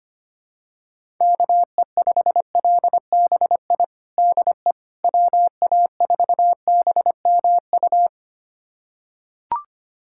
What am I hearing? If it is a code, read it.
Morse code, heard as a keyed tone, about 25 wpm, 700 Hz. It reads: KE5LBI DE WA4BMU
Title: USA Callsign DE Callsign - 6 Characters - 25wpm